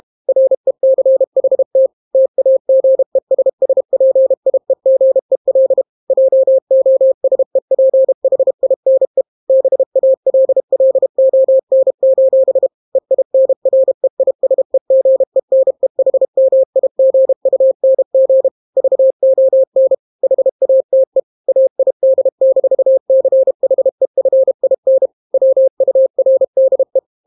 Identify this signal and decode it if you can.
Morse code, heard as a keyed tone, about 32 wpm, 530 Hz. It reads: RECHT TAGESSPIEGEL JOSEPHINE BALLON: EINREISEGENEHMIGUNG VON HATE AID-CHEFIN WURDE